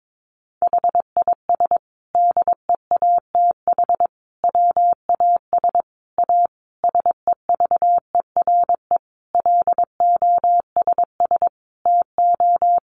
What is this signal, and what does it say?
Morse code, heard as a keyed tone, about 22 wpm, 700 Hz. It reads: HIS DEATH WAS A SEVERE LOSS TO
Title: Sentences from Top 1400 Words 22wpm